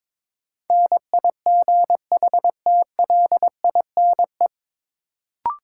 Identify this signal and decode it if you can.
Morse code, heard as a keyed tone, about 21 wpm, 700 Hz. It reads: NIGHTLINE
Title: Sets of 5 Words - New Word Drill - 22wpm